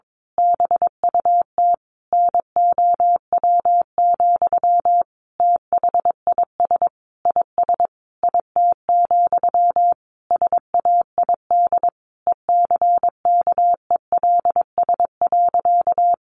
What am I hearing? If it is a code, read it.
Morse code, heard as a keyed tone, about 22 wpm, 700 Hz. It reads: BUT NOW, THIS IS IT, SAID ECKELS.
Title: A Sound Of Thunder 22wpm